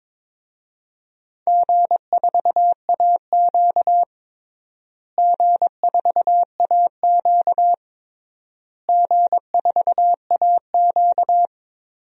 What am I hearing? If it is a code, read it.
Morse code, heard as a keyed tone, about 22 wpm, 700 Hz. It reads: G4AQ G4AQ G4AQ
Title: DXCC-100 Call Signs - 4 Characters - Repeated 3x - 22wpm